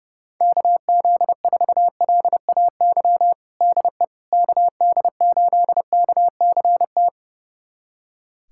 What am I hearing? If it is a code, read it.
Morse code, heard as a keyed tone, about 30 wpm, 700 Hz. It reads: KZ4LAY DE KD8KCT
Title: USA Callsign DE Callsign - 6 Characters - 30wpm